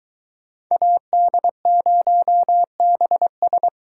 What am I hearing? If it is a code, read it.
Morse code, heard as a keyed tone, about 22 wpm, 700 Hz. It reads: AD0BS